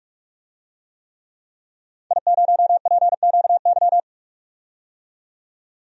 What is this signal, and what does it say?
Morse code, heard as a keyed tone, about 45 wpm, 700 Hz. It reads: I0PQY
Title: DXCC-100 Call Signs - 5 Characters - 45wpm